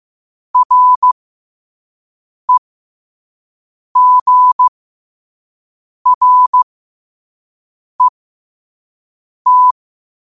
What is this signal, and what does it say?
Morse code, heard as a keyed tone, about 15 wpm, spaced as for 5 wpm, 1000 Hz. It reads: REGRET
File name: qotdfiles / 5wpm